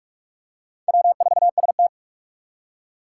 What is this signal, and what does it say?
Morse code, heard as a keyed tone, about 44 wpm, 700 Hz. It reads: W4ST
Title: Four Challenging Letter-Number Combinations 45wpm